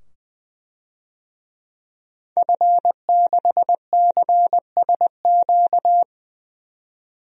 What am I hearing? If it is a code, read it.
Morse code, heard as a keyed tone, about 20 wpm, 700 Hz. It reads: F6CSQ
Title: DXCC-100 Call Signs - 5 Characters - 20wpm